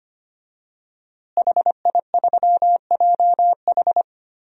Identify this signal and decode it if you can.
Morse code, heard as a keyed tone, about 25 wpm, 700 Hz. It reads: HI3JH
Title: DXCC-100 Call Signs - 5 Characters - Repeated 2x - 25wpm